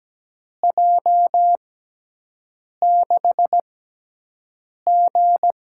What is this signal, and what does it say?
Morse code, heard as a keyed tone, about 17 wpm, spaced as for 6 wpm, 700 Hz. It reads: J6G
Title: USA Call Signs - 5 Characters - 6x Character Spacing - 17wpm